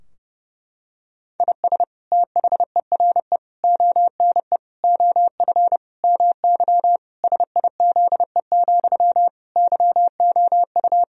Morse code, heard as a keyed tone, about 30 wpm, 700 Hz. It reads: IS THERE ONE OF MY SIZE, YOU
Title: Sentences from Top 1000 Words - Review Part 2 - 30wpm